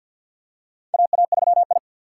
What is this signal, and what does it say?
Morse code, heard as a keyed tone, about 47 wpm, 700 Hz. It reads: AA3I
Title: USA Call Signs - 4 Characters - 3x Character Spacing - 50wpm